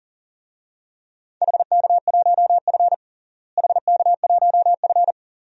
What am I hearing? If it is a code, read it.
Morse code, heard as a keyed tone, about 40 wpm, 700 Hz. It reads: HK1F HK1F
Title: DXCC-100 Call Signs - 4 Characters - Repeated 3x - 40wpm